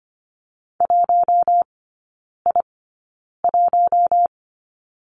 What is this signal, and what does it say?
Morse code, heard as a keyed tone, about 25 wpm, spaced as for 9 wpm, 700 Hz. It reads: 1I1
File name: Lesson 14e - Number 2 - Three Characters 25wpm